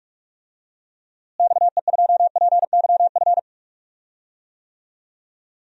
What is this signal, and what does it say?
Morse code, heard as a keyed tone, about 46 wpm, 700 Hz. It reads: XE2PYF
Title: DXCC-100 Call Signs - 6 Characters - Repeated 2x - 45wpm